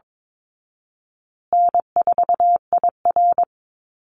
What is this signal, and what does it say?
Morse code, heard as a keyed tone, about 22 wpm, 700 Hz. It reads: N4IR
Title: Easy Call Signs Repeated 2x - 22wpm